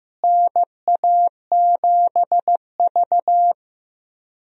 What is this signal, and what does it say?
Morse code, heard as a keyed tone, about 15 wpm, 700 Hz. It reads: NA7V
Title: USA Call Signs - 4 Characters - Repeated 2x - 15wpm